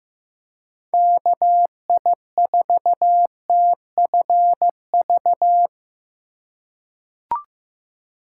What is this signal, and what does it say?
Morse code, heard as a keyed tone, about 15 wpm, 700 Hz. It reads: KI4TFV
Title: USA Call Signs - 6 Characters - 8x Character Spacing - 15wpm